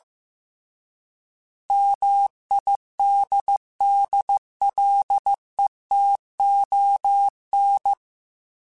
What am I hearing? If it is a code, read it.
Morse code, heard as a keyed tone, about 15 wpm, 800 Hz. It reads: MIDDLETON